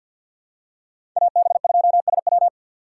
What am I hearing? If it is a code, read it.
Morse code, heard as a keyed tone, about 47 wpm, 700 Hz. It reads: AB2SW
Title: USA Call Signs - 5 Characters - 4x Character Spacing - 50wpm